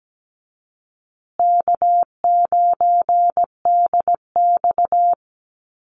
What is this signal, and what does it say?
Morse code, heard as a keyed tone, about 17 wpm, 700 Hz. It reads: K9DX
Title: Easy Call Signs Repeated 2x - 17wpm